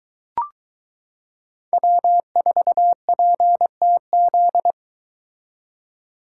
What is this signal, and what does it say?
Morse code, heard as a keyed tone, about 22 wpm, 700 Hz. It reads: W4PTZ